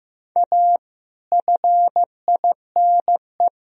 Morse code, heard as a keyed tone, about 14 wpm, 700 Hz. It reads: A FINE
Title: Sentences from Top 1000 Words - Review Part 2 - 15wpm